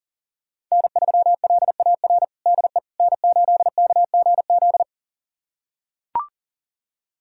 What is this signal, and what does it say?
Morse code, heard as a keyed tone, about 40 wpm, 700 Hz. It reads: N3LAR DE N8KGZ
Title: USA Callsign DE Callsign - 5 Characters - 40wpm